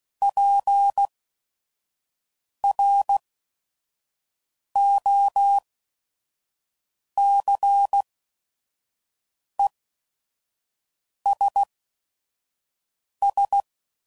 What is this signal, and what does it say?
Morse code, heard as a keyed tone, about 16 wpm, spaced as for 5 wpm, 800 Hz. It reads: PROCESS
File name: CW Code Groups @5WPM / Technical Subjects / AND Gate